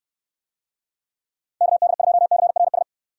Morse code, heard as a keyed tone, about 69 wpm, 700 Hz. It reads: KD2CRS